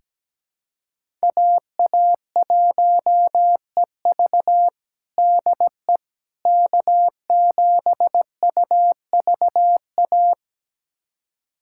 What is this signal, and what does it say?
Morse code, heard as a keyed tone, about 17 wpm, 700 Hz. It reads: AA1EV DE K7UVA
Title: USA Callsign DE Callsign - 5 Characters - 17wpm